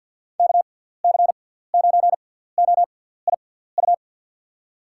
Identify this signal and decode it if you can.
Morse code, heard as a keyed tone, about 47 wpm, 700 Hz. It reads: K C 9 O I U
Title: USA Call Signs - 6x Character Spacing - 50wpm